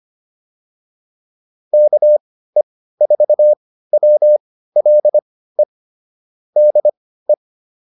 Morse code, heard as a keyed tone, about 25 wpm, spaced as for 15 wpm, 600 Hz. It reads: KE4WLE DE
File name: QSO110 15